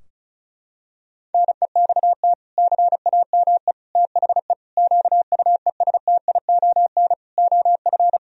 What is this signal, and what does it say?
Morse code, heard as a keyed tone, about 36 wpm, 700 Hz. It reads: NEXT CAME THE QUESTION OF
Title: Sentences from Top 1000 Words - Review Part 4 - 35wpm